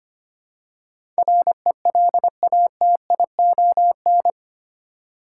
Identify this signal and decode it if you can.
Morse code, heard as a keyed tone, about 25 wpm, 700 Hz. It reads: RELATION
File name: Eight Letter Words - 25wpm